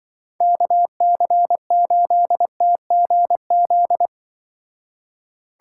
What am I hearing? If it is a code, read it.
Morse code, heard as a keyed tone, about 24 wpm, 700 Hz. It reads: KC8TGZ
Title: Speed Racing - USA 2x3 Call Signs 20wpm